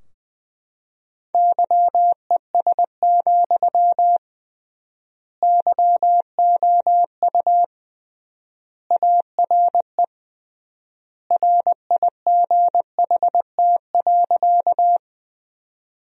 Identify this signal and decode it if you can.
Morse code, heard as a keyed tone, about 20 wpm, 700 Hz. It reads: YES, YOU ARE RIGHT.
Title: 3x Word-Spacing - Sentences from Top 500 Words - 20wpm